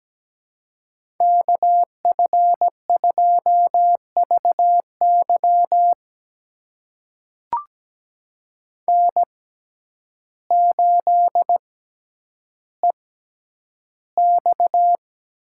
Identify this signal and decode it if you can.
Morse code, heard as a keyed tone, about 17 wpm, spaced as for 9 wpm, 700 Hz. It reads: KF2VY N8EX
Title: USA Call Signs - 5 Characters - 6x Character Spacing - 17wpm